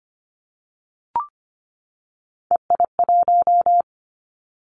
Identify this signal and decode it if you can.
Morse code, heard as a keyed tone, about 25 wpm, 700 Hz. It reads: EI1
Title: Callsign Prefixes for Top 50 DXCC - 25wpm